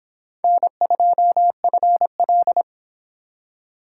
Morse code, heard as a keyed tone, about 26 wpm, 700 Hz. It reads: N2FL